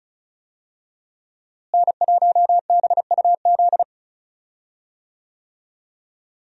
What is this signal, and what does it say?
Morse code, heard as a keyed tone, about 34 wpm, 700 Hz. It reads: N1BUZ
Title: USA Call Signs - 5 Characters - Repeated 2x - 35wpm